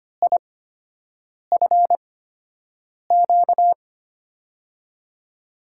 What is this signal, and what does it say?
Morse code, heard as a keyed tone, about 25 wpm, spaced as for 7 wpm, 700 Hz. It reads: IFQ
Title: USA Call Signs - 5 Characters - 8x Character Spacing - 25wpm